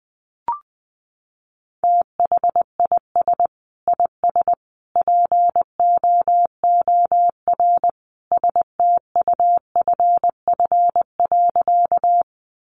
Morse code, heard as a keyed tone, about 20 wpm, 700 Hz. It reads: THIS IS POOR STUFF.
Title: Sentences from Top 1400 Words 20wpm